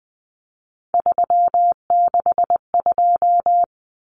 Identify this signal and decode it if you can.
Morse code, heard as a keyed tone, about 20 wpm, 700 Hz. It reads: 362
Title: Common Cut Numbers - 3-Digit 20wpm